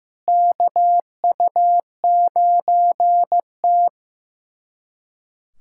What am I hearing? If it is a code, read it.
Morse code, heard as a keyed tone, about 15 wpm, 700 Hz. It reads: KU9T
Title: USA Callsign DE Callsign - 4 Characters - 15wpm